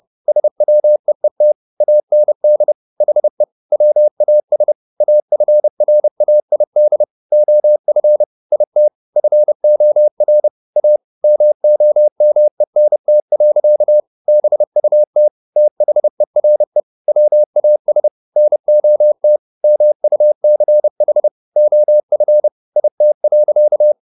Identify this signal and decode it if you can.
Morse code, heard as a keyed tone, about 30 wpm, 600 Hz. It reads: SWEET AND HE WAS AFRAID OF IT FOR A MOMENT. BUT THERE WAS NOT MUCH OF IT.